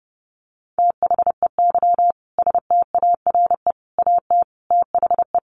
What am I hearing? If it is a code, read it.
Morse code, heard as a keyed tone, about 30 wpm, 700 Hz. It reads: THEY STARE AT THE
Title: Easy Phrases 30wpm